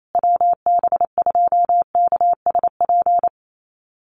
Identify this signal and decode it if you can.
Morse code, heard as a keyed tone, about 28 wpm, 700 Hz. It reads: WB2KSP
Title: USA Call Signs Repeated 3x 28wpm